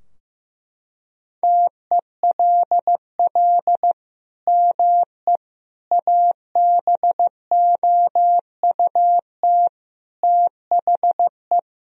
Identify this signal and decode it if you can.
Morse code, heard as a keyed tone, about 15 wpm, 700 Hz. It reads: TELL ME ABOUT THE
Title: Sentences from Top 1000 Words - Review Part 2 - 15wpm